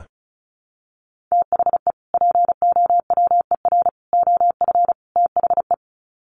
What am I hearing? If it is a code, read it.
Morse code, heard as a keyed tone, about 34 wpm, 700 Hz. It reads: THE POWER OF THE
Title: Sets of 4 Words 35wpm